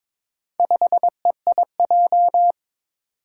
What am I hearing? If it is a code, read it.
Morse code, heard as a keyed tone, about 22 wpm, 700 Hz. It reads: 5EIJ
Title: Four Challenging Letter-Number Combinations 22wpm